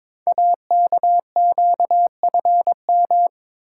Morse code, heard as a keyed tone, about 22 wpm, 700 Hz. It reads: AKQFM
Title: Code Groups - 5 Letter:Numbers - 22wpm